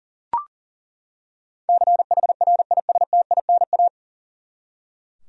Morse code, heard as a keyed tone, about 40 wpm, 700 Hz. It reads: CHRISTINA
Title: USA Common Names - 40wpm